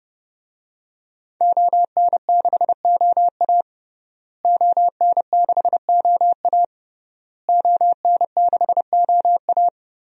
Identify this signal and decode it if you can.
Morse code, heard as a keyed tone, about 32 wpm, 700 Hz. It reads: ON6OA ON6OA ON6OA
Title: DXCC-100 Call Signs - 5 Characters - Repeated 3x - 30wpm